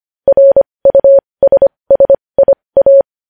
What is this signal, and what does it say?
Morse code, heard as a keyed tone, about 25 wpm, spaced as for 20 wpm, 550 Hz. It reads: RUSSIA
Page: news-headlines-in-morse-code-at-20-wpm